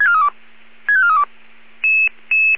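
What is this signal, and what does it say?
Five-tone selective call (ZVEI), heard as quick runs of five tones, tones of about 70 ms. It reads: 65432
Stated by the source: ZVEI five-tone call 12345, received by SDR.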